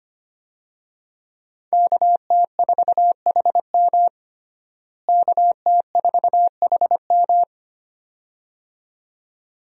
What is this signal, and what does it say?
Morse code, heard as a keyed tone, about 25 wpm, 700 Hz. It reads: KT4HM KT4HM
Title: USA Call Signs - 5 Characters - Repeated 2x - 25wpm